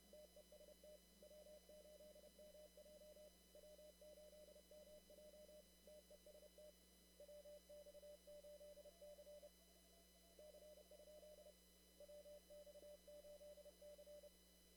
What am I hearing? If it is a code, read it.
Morse code, heard as a keyed tone, about 31 wpm, 600 Hz, with noise. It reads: TEST W8MJ W8MJ TEST WX8C C? WX8C
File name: CW / kt5j wpx cw-op-k5pi